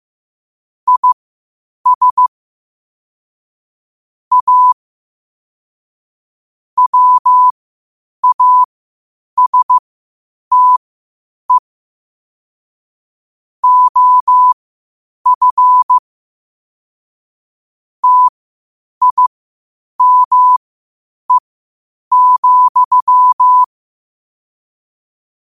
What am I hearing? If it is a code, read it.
Morse code, heard as a keyed tone, about 15 wpm, spaced as for 8 wpm, 1000 Hz. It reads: IS A WASTE OF TIME,
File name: qotdfiles / 8wpm